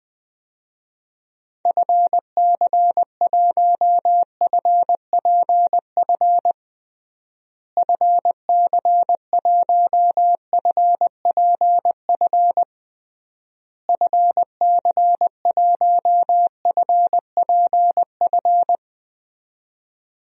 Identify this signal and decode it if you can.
Morse code, heard as a keyed tone, about 21 wpm, 700 Hz. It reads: FC1FPF FC1FPF FC1FPF
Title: DXCC-100 Call Signs - 6 Characters - Repeated 3x - 20wpm